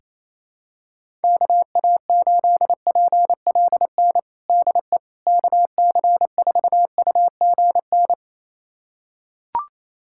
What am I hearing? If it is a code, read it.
Morse code, heard as a keyed tone, about 27 wpm, 700 Hz. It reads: KA8PLN DE KC4UGN
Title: USA Callsign DE Callsign 28wpm